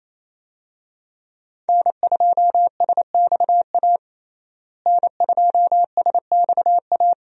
Morse code, heard as a keyed tone, about 28 wpm, 700 Hz. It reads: N2SXA N2SXA
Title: USA Call Signs - 5 Characters - Repeated 2x - 28wpm